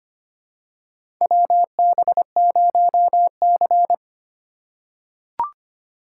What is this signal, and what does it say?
Morse code, heard as a keyed tone, about 25 wpm, 700 Hz. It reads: WB0C
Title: USA Call Signs - 2x Character Spacing - 25wpm